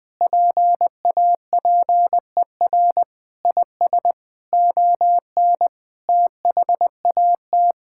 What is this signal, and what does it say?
Morse code, heard as a keyed tone, about 21 wpm, 700 Hz. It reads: PAPER IS ON THAT
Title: Sentences from Top 1000 Words - Review Part 6 - 20wpm